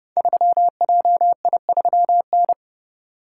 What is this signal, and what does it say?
Morse code, heard as a keyed tone, about 30 wpm, 700 Hz. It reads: 3JI3N
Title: Code Groups - 5 Letter:Numbers - 30wpm